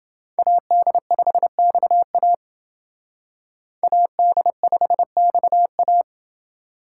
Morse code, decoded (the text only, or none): AD5XA AD5XA